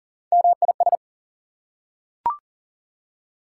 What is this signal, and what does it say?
Morse code, heard as a keyed tone, about 40 wpm, 700 Hz. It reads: MIS
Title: Prefixes and Suffixes 40wpm